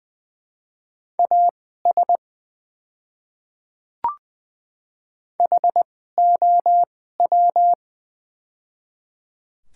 Morse code, heard as a keyed tone, about 20 wpm, spaced as for 14 wpm, 700 Hz. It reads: AS HOW
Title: 2x Character Spacing - Top 100 Words - 20wpm